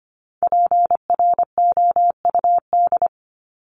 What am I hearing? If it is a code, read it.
Morse code, heard as a keyed tone, about 25 wpm, 700 Hz. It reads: PROUD
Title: Top 1501-1750 Words 25wpm